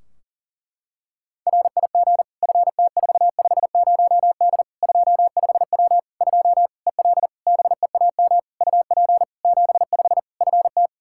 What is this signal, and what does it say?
Morse code, heard as a keyed tone, about 40 wpm, 700 Hz. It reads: RIG FT450D 25W 2 EL BEAM UP 75 FT
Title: Practice QSO - Exchange 2 - Info - 40wpm